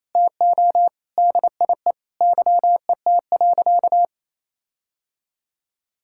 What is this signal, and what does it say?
Morse code, heard as a keyed tone, about 27 wpm, 700 Hz. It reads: TO DIE YET.
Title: Sentences from Top 1600 Words 28wpm